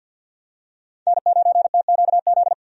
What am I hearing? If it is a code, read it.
Morse code, heard as a keyed tone, about 52 wpm, 700 Hz. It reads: N9TQZ
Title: USA Call Signs - 5 Characters - 50wpm